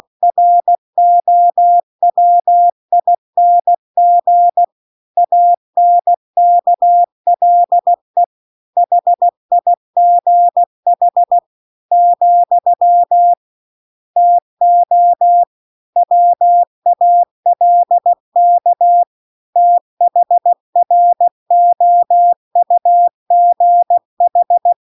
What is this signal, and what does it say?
Morse code, heard as a keyed tone, about 16 wpm, 700 Hz. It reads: ROWING ANKLE HIGH , TO WALK THROUGH